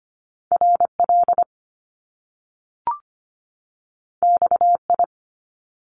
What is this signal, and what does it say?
Morse code, heard as a keyed tone, about 25 wpm, 700 Hz. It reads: RL XI
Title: Top Two Letter Combinations in Words 25wpm